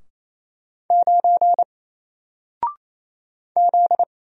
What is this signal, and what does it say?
Morse code, heard as a keyed tone, about 28 wpm, 700 Hz. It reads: 9 Z
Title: Single Letter-Number Alt 28wpm